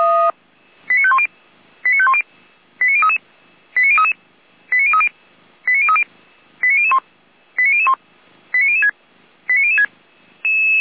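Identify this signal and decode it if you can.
Five-tone selective call (ZVEI), heard as quick runs of five tones, tones of about 70 ms. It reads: 89510 89510 89033 89033 89030 89030 89001 89001 89006 89006
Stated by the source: ZVEI five-tone call received by SDR.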